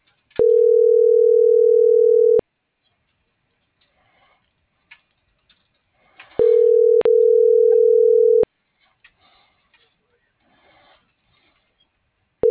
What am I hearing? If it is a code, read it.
Ringback tone (North America), 440 + 480 Hz.